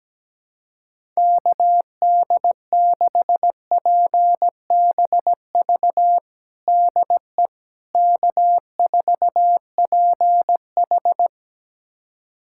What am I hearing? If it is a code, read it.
Morse code, heard as a keyed tone, about 17 wpm, 700 Hz. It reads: KD6PBV DE K4PH
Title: USA Callsign DE Callsign 17wpm